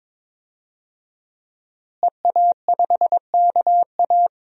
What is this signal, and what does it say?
Morse code, heard as a keyed tone, about 22 wpm, 700 Hz. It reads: EA5KA
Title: DXCC-100 Call Signs - 5 Characters - Repeated 3x - 22wpm